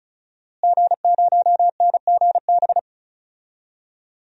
Morse code, heard as a keyed tone, about 34 wpm, 700 Hz. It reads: G0NGB
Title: DXCC-100 Call Signs - 5 Characters - Repeated 3x - 35wpm